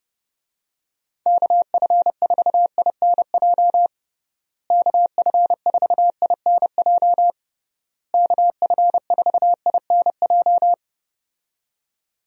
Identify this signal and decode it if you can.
Morse code, heard as a keyed tone, about 30 wpm, 700 Hz. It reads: KF4INJ KF4INJ KF4INJ
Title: USA Call Signs - 6 Characters - Repeated 3x - 30wpm